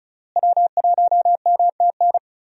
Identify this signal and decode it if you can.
Morse code, heard as a keyed tone, about 36 wpm, 700 Hz. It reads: W1MTN
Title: USA Call Signs - 5 Characters - Repeated 3x - 35wpm